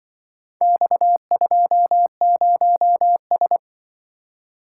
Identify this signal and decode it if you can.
Morse code, heard as a keyed tone, about 26 wpm, 700 Hz. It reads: X20S